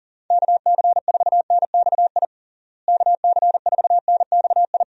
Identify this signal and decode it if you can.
Morse code, heard as a keyed tone, about 40 wpm, 700 Hz. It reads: KC4NXI KC4NXI
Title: USA Call Signs - 6 Characters - Repeated 2x - 40wpm